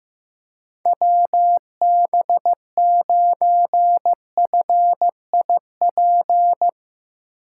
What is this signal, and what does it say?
Morse code, heard as a keyed tone, about 15 wpm, 700 Hz. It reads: WB9FIP
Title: K1USN SST QSO Practice - 15wpm